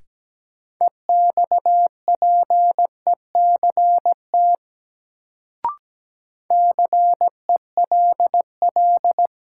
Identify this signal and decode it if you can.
Morse code, heard as a keyed tone, about 17 wpm, 700 Hz. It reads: EXPECT CELL
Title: Sentences from Top 1100 Words - New Word Drill Alt - 17wpm